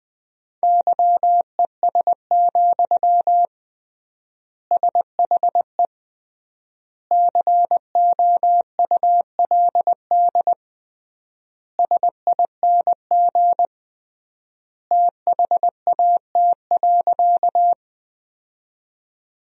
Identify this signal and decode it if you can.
Morse code, heard as a keyed tone, about 20 wpm, 700 Hz. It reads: YES, SHE COULD SING THAT.
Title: 3x Word-Spacing - Sentences from Top 500 Words - 20wpm